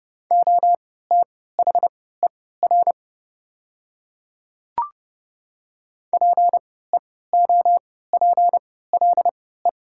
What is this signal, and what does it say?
Morse code, heard as a keyed tone, about 30 wpm, spaced as for 17 wpm, 700 Hz. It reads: OTHER PEOPLE
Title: 3x Character Spacing - Top 100 Words - 30wpm